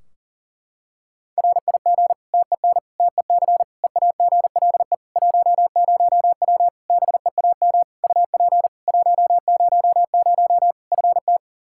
Morse code, heard as a keyed tone, about 40 wpm, 700 Hz. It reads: RIG TEN TEC EAGLE 10W BEAM UP 100 FT
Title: Practice QSO - Exchange 2 - Info - 40wpm